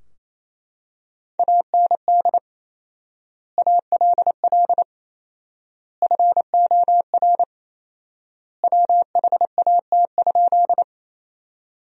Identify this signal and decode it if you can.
Morse code, heard as a keyed tone, about 28 wpm, 700 Hz. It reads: AND ALL FOR WHAT?
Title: 4x Word-Spacing - Sentences from Top 100 Words - 28wpm